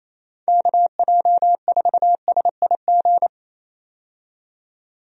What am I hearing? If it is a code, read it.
Morse code, heard as a keyed tone, about 28 wpm, 700 Hz. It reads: KJ4SIG
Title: USA Call Signs - 6 Characters - 28wpm